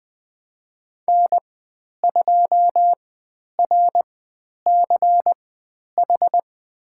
Morse code, heard as a keyed tone, about 20 wpm, spaced as for 10 wpm, 700 Hz. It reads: N2RCH
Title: 095 Lesson 16g - Letter c - Call Signs 20wpm